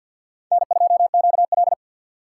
Morse code, heard as a keyed tone, about 53 wpm, 700 Hz. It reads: N2QL
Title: USA Call Signs - 4 Characters - 2x Character Spacing - 50wpm